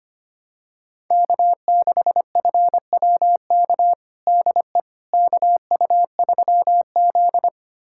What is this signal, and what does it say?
Morse code, heard as a keyed tone, about 25 wpm, 700 Hz. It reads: K6FWK DE KU3Z
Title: USA Callsign DE Callsign 25wpm